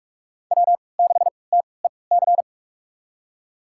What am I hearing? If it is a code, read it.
Morse code, heard as a keyed tone, about 46 wpm, spaced as for 25 wpm, 700 Hz. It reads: W6TEC
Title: USA Call Signs - 5 Characters - 3x Character Spacing - 45wpm